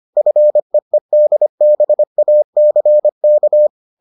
Morse code, heard as a keyed tone, about 25 wpm, 600 Hz. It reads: FEEDBACK